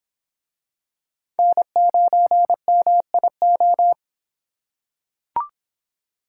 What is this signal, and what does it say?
Morse code, heard as a keyed tone, about 24 wpm, 700 Hz. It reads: N9MIO